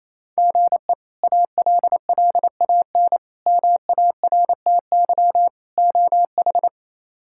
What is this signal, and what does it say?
Morse code, heard as a keyed tone, about 28 wpm, 700 Hz. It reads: GE ALLAN MARTY OH
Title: K1USN SST QSO Practice - 28wpm